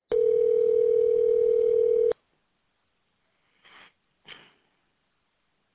Ringback tone (North America), 440 + 480 Hz.